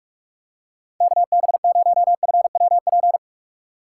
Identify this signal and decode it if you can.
Morse code, heard as a keyed tone, about 47 wpm, 700 Hz. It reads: KB0FWP
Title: USA Call Signs - 6 Characters - 3x Character Spacing - 45wpm